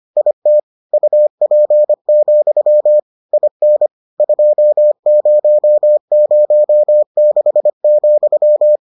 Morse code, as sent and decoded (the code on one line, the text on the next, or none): .. - / ..- .--. --..-- / .. -. / ..--- ----- ----- -.... --..--
IT UP, IN 2006,